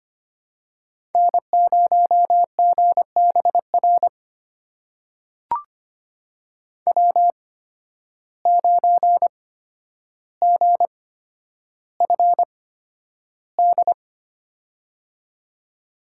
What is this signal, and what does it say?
Morse code, heard as a keyed tone, about 25 wpm, spaced as for 11 wpm, 700 Hz. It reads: N0GBR W9GFD
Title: USA Call Signs - 8x Character Spacing - 25wpm